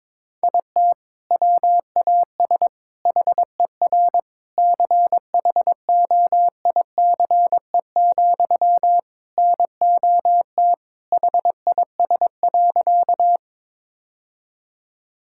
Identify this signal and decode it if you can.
Morse code, heard as a keyed tone, about 22 wpm, 700 Hz. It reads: IT WAS HER CHOICE, NOT HIS.
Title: Sentences from Top 1000 Words - Review Part 6 - 22wpm